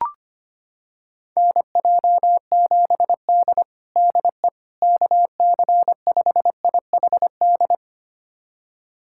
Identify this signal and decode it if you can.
Morse code, heard as a keyed tone, about 25 wpm, 700 Hz. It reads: NJ7D DE KC5IHD
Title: USA Callsign DE Callsign 25wpm